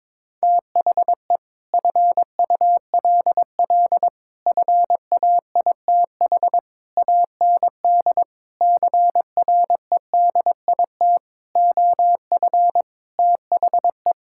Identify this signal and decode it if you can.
Morse code, heard as a keyed tone, about 22 wpm, 700 Hz. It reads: THE FULL FAITH AND CREDIT OF THE
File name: Sets of 9 Words - 22wpm